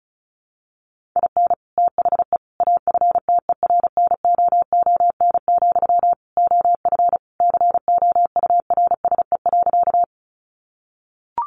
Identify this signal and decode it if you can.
Morse code, heard as a keyed tone, about 34 wpm, 700 Hz. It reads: IN THE AFTERNOON, OF COURSE.
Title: Sentences from Top 1500 Words 35wpm